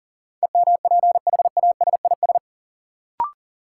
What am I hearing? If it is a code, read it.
Morse code, heard as a keyed tone, about 40 wpm, 700 Hz. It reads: EMPHASIS
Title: Sentences from Top 1400 Words - New Word Drill - 40wpm